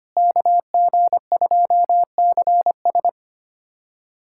Morse code, heard as a keyed tone, about 25 wpm, 700 Hz. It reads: KG2CS
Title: USA Callsign DE Callsign - 5 Characters - 25wpm